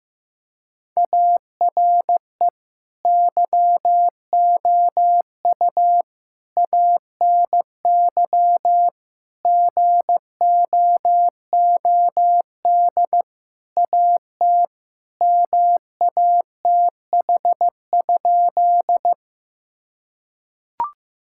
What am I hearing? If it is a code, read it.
Morse code, heard as a keyed tone, about 15 wpm, 700 Hz. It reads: ARE YOU ANY GOOD AT MATH?
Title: Sentences from Top 1000 Words - Review Part 6 - 15wpm